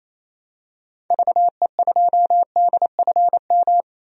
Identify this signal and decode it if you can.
Morse code, heard as a keyed tone, about 28 wpm, 700 Hz. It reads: VE2DFM
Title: DXCC-100 Call Signs - 6 Characters - 28wpm